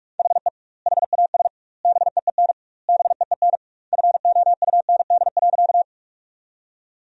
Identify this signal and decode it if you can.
Morse code, heard as a keyed tone, about 41 wpm, 700 Hz. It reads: HE HAS BEEN BEEN FOUND.
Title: 2x Word-Spacing - Sentences from Top 300 Words - 45wpm